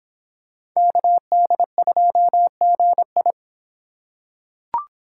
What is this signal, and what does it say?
Morse code, heard as a keyed tone, about 26 wpm, 700 Hz. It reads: KD2GI